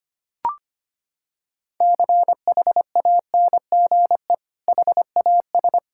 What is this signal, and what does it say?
Morse code, heard as a keyed tone, about 25 wpm, 700 Hz. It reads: CHANGE HAS
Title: Sentences from Top 1600 Words 25wpm